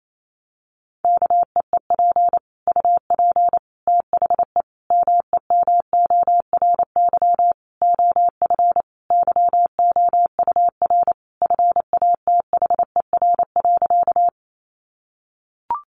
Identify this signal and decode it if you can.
Morse code, heard as a keyed tone, about 28 wpm, 700 Hz. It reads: KEEP UP THE MEMORY OF YOUR FATHER.
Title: Sentences from Top 1300 Words 28wpm